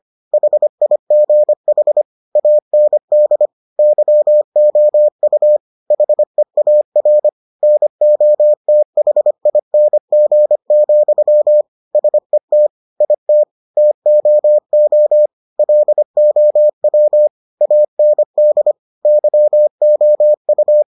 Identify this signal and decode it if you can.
Morse code, heard as a keyed tone, about 25 wpm, 600 Hz. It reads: HIGH AND YOU HEAR NOTHING, SET IT TOO LOW AND YOU